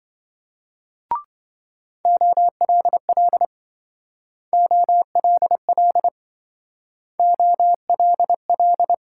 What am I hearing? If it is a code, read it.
Morse code, heard as a keyed tone, about 27 wpm, 700 Hz. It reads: OLL OLL OLL